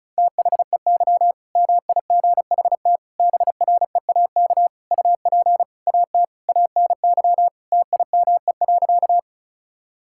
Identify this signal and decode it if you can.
Morse code, heard as a keyed tone, about 34 wpm, 690 Hz. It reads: THEY MIGHT BREAK UP AT ANY TIME.
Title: Sentences from Top 1000 Words - Review Part 6 - 35wpm